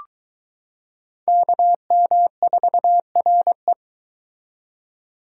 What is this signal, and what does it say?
Morse code, heard as a keyed tone, about 22 wpm, 700 Hz. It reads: KM4RE